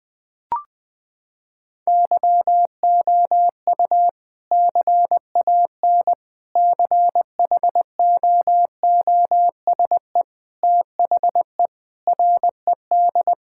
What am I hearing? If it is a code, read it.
Morse code, heard as a keyed tone, about 21 wpm, 700 Hz. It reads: YOU CAN CHOOSE THE RED
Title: Sentences with 2-Word Phrases - 20wpm